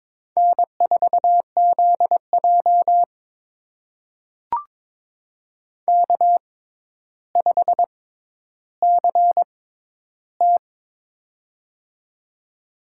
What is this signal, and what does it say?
Morse code, heard as a keyed tone, about 22 wpm, spaced as for 11 wpm, 700 Hz. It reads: N4ZJ K5CT
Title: USA Call Signs - 4 Characters - 6x Character Spacing - 22wpm